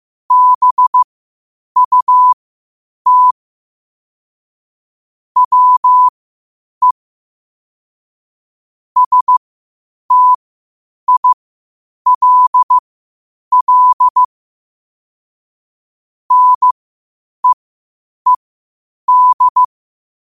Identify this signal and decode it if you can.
Morse code, heard as a keyed tone, about 15 wpm, spaced as for 8 wpm, 1000 Hz. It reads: BUT WE STILL NEED